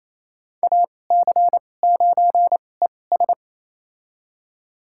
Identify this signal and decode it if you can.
Morse code, heard as a keyed tone, about 28 wpm, spaced as for 20 wpm, 700 Hz. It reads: AC9ES
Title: USA Call Signs - 5 Characters - 2x Character Spacing - 28wpm